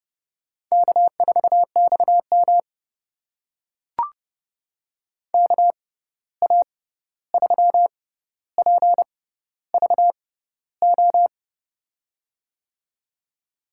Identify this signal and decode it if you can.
Morse code, heard as a keyed tone, about 30 wpm, spaced as for 14 wpm, 700 Hz. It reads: K4XM KA3PVO
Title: USA Call Signs - 6x Character Spacing - 30wpm